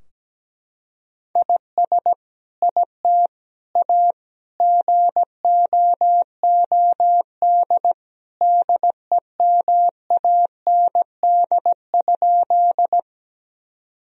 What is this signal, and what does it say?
Morse code, heard as a keyed tone, about 17 wpm, 700 Hz. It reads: IS IT A GOOD DEMAND?
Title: Sentences from Top 1000 Words - Review Part 4 - 17wpm